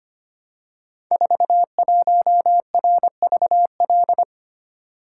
Morse code, heard as a keyed tone, about 25 wpm, 700 Hz. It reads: 41RVL
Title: Code Groups - 5 Letter:Numbers - 25wpm